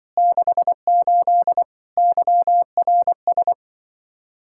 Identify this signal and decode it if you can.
Morse code, heard as a keyed tone, about 23 wpm, 700 Hz. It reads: 68 YRS
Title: Speed-Racing Common QSO Phrases - 20wpm